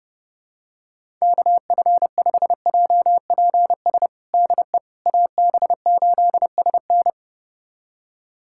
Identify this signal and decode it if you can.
Morse code, heard as a keyed tone, about 30 wpm, 700 Hz. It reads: KF5JPS DE AB8SN
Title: USA Callsign DE Callsign 30wpm